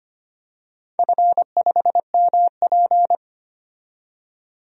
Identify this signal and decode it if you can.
Morse code, heard as a keyed tone, about 25 wpm, 700 Hz. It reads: F5MP
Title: DXCC-100 Call Signs - 4 Characters - 25wpm